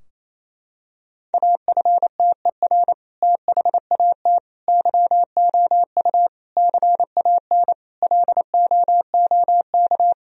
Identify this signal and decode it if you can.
Morse code, heard as a keyed tone, about 28 wpm, 700 Hz. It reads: AFTER THAT YOU CAN LOOK
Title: Sentences from Top 1000 Words - Review Part 2 - 28wpm